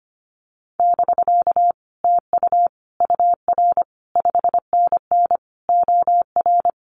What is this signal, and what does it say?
Morse code, heard as a keyed tone, about 25 wpm, 700 Hz. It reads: <BK> TU UR 5NN OR
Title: Common QSO Phrases Encore 25wpm